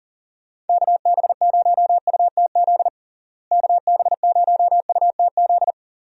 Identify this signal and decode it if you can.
Morse code, heard as a keyed tone, about 40 wpm, 700 Hz. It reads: KB0UTZ KB0UTZ
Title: USA Call Signs - 6 Characters - Repeated 3x - 40wpm